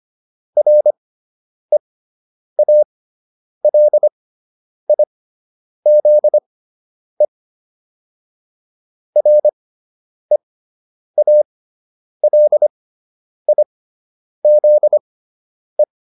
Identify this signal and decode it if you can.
Morse code, heard as a keyed tone, about 25 wpm, spaced as for 9 wpm, 600 Hz. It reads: REALIZE REALIZE